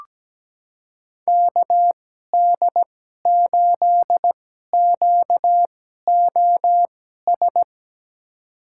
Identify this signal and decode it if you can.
Morse code, heard as a keyed tone, about 17 wpm, spaced as for 12 wpm, 700 Hz. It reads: KD8QOS
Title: USA Call Signs - 6 Characters - 2x Character Spacing - 17wpm